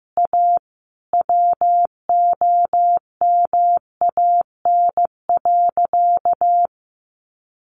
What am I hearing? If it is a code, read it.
Morse code, heard as a keyed tone, about 15 wpm, 700 Hz. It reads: A WOMAN.
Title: Sentences from Top 1100 Words 15wpm